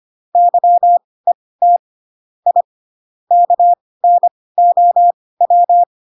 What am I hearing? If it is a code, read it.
Morse code, heard as a keyed tone, about 25 wpm, spaced as for 18 wpm, 700 Hz. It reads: YET I KNOW
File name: Pinocchio / Pinocchio ch2 12 25 ik1hge